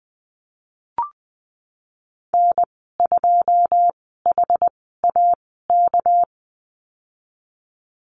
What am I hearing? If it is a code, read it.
Morse code, heard as a keyed tone, about 20 wpm, spaced as for 15 wpm, 700 Hz. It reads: N2HAK
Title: 200 Lesson 32g - Letter k - Call Signs 20wpm